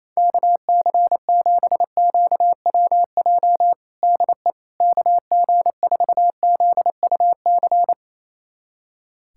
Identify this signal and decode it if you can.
Morse code, heard as a keyed tone, about 28 wpm, 700 Hz. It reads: KC7QWJ DE KG4ZUC
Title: USA Callsign DE Callsign - 6 Characters - 28wpm